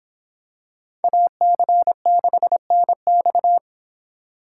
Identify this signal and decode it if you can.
Morse code, heard as a keyed tone, about 26 wpm, 700 Hz. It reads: AC6NX